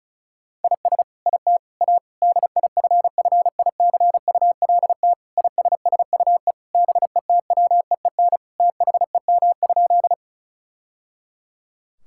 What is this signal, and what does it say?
Morse code, heard as a keyed tone, about 36 wpm, 700 Hz. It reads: IS IT A DIFFICULT ISSUE BETWEEN THEM?
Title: Sentences from Top 1000 Words - Review Part 6 - 35wpm